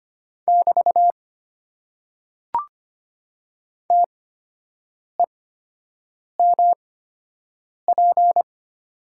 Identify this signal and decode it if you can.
Morse code, heard as a keyed tone, about 25 wpm, spaced as for 7 wpm, 700 Hz. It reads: <BT> TEMP
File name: Top 90 QSO Elements - 8x Character Spacing - 25wpm